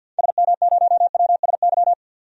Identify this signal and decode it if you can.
Morse code, heard as a keyed tone, about 48 wpm, 700 Hz. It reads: SM0WSY
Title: DXCC-100 Call Signs - 6 Characters - 50wpm